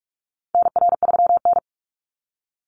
Morse code, heard as a keyed tone, about 44 wpm, 700 Hz. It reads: NR3N
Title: Call Signs - 45wpm